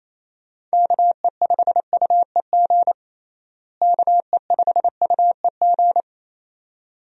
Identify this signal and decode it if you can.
Morse code, heard as a keyed tone, about 28 wpm, 700 Hz. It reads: KE5UEG KE5UEG
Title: USA Call Signs - 6 Characters - Repeated 2x - 28wpm